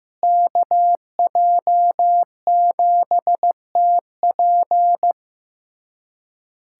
Morse code, heard as a keyed tone, about 15 wpm, 700 Hz. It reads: KJ7TP
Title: USA Callsign DE Callsign - 5 Characters - 15wpm